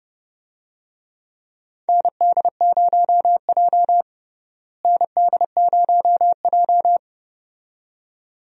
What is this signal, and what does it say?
Morse code, heard as a keyed tone, about 32 wpm, 700 Hz. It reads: ND0J ND0J
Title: USA Call Signs - 4 Characters - Repeated 2x - 30wpm